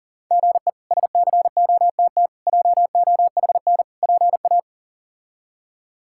Morse code, heard as a keyed tone, about 40 wpm, 700 Hz. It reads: GE SCOTT JOHN PA
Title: K1USN SST QSO Practice - 40wpm